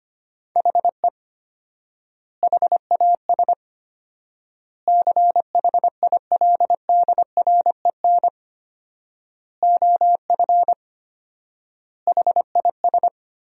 Morse code, heard as a keyed tone, about 25 wpm, 700 Hz. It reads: HE HAS CHILDREN OF HIS
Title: 4x Word-Spacing - Sentences from Top 300 Words - 25wpm